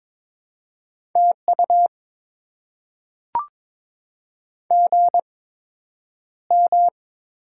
Morse code, heard as a keyed tone, about 22 wpm, spaced as for 9 wpm, 700 Hz. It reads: TU GM
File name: Top 20 QSO Elements - 8x Character Spacing - 22wpm